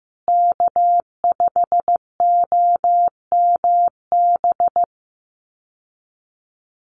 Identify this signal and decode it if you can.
Morse code, heard as a keyed tone, about 15 wpm, 700 Hz. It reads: K5OMB
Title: USA Call Signs Repeated 2x 15wpm